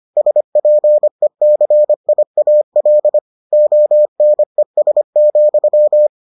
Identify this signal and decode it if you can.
Morse code, heard as a keyed tone, about 25 wpm, 600 Hz. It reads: SPECIAL ONES,